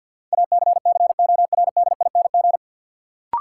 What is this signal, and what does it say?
Morse code, heard as a keyed tone, about 50 wpm, 700 Hz. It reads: ACCORDING
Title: Sets of 2 Words - New Word Drill - 50wpm